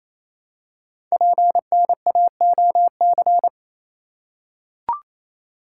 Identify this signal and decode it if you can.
Morse code, heard as a keyed tone, about 28 wpm, 700 Hz. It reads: PNAOC
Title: Code Groups - 5 Letters - 28wpm